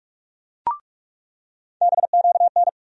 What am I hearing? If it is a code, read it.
Morse code, heard as a keyed tone, about 42 wpm, 700 Hz. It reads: BQN
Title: USA Call Sign Postfixes - 3 Letters 45wpm